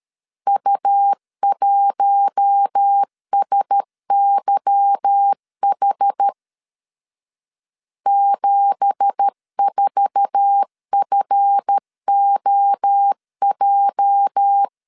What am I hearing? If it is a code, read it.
Morse code, heard as a keyed tone, about 13 wpm, 800 Hz. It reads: U1SYH 74FOJ